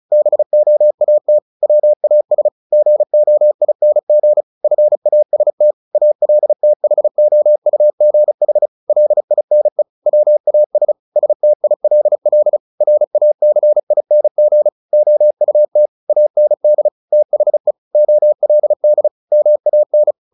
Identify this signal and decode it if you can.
Morse code, heard as a keyed tone, about 34 wpm, 600 Hz. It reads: BOAT WAS GOING FAST ALTHOUGH LINE WAS STILL RACING OUT AND THE OLD MAN
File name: the old man and the sea 2 35wpm